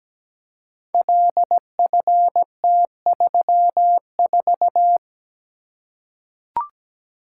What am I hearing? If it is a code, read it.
Morse code, heard as a keyed tone, about 17 wpm, 700 Hz. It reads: LFT34
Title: Code Groups - 5 Letter:Numbers - 17wpm